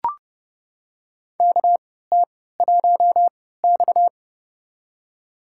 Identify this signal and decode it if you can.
Morse code, heard as a keyed tone, about 31 wpm, spaced as for 17 wpm, 700 Hz. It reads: KT1X
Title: USA Call Signs - 4 Characters - 3x Character Spacing - 30wpm